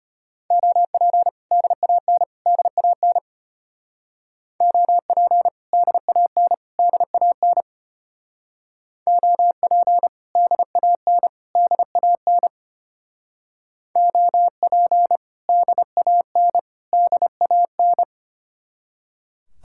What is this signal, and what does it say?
Morse code, heard as a keyed tone, about 32 wpm, 700 Hz. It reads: OP DAN DAN OP DAN DAN OP DAN DAN OP DAN DAN